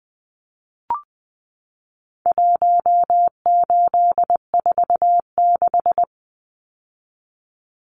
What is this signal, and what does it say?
Morse code, heard as a keyed tone, about 20 wpm, 700 Hz. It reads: 1846
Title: Numbers - Contest Style - 4-Digit 20wpm